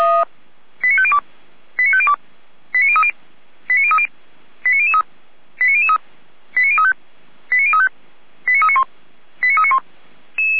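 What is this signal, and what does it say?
Five-tone selective call (ZVEI), heard as quick runs of five tones, tones of about 70 ms. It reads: 89502 89502 89030 89030 89003 89003 89035 89035 89391 89391
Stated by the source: ZVEI five-tone call 89062 received on SDR.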